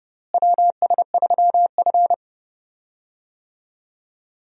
Morse code, heard as a keyed tone, about 30 wpm, 700 Hz. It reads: WS3F
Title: USA Call Signs - 4 Characters - Repeated 2x - 30wpm